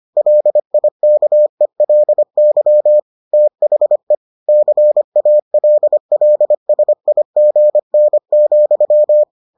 Morse code, heard as a keyed tone, about 25 wpm, 600 Hz. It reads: LIKELY THE CALLSIGN,